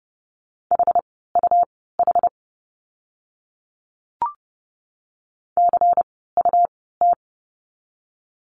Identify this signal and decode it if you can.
Morse code, heard as a keyed tone, about 30 wpm, spaced as for 17 wpm, 700 Hz. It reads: HUH CUT
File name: Lesson 17f - Letter u - Words 30wpm